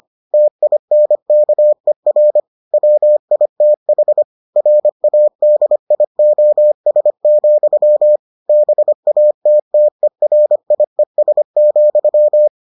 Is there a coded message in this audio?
Yes: Morse code from 0.3 to 12.5 s, heard as a keyed tone, about 25 wpm, 600 Hz. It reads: TINKER WITH RADIOS, BATTERIES,